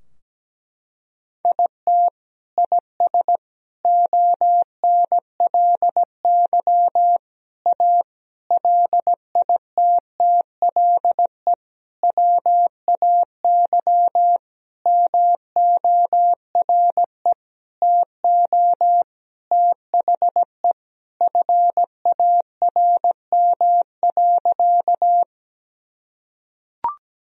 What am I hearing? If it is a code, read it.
Morse code, heard as a keyed tone, about 17 wpm, 700 Hz. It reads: IT IS ONLY A LITTLE WAY MORE TO THE FARM.
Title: Sentences from Top 1000 Words - Review Part 2 - 17wpm